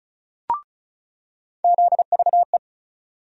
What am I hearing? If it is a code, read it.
Morse code, heard as a keyed tone, about 36 wpm, 700 Hz. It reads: ZVE
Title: USA Call Sign Postfixes - 3 Letters 35wpm